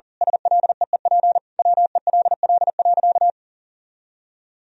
Morse code, heard as a keyed tone, about 40 wpm, 700 Hz. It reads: SLEEP WELL.
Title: Sentences from Top 500 Words and Abbreviations 40wpm